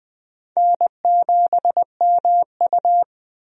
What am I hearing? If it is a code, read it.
Morse code, heard as a keyed tone, about 20 wpm, 700 Hz. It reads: N7MU
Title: USA Call Signs - 4 Characters - 20wpm